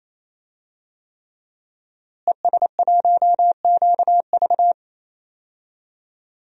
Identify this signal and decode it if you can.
Morse code, heard as a keyed tone, about 28 wpm, 700 Hz. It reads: ES1QV
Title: DXCC-100 Call Signs - 5 Characters - Repeated 3x - 28wpm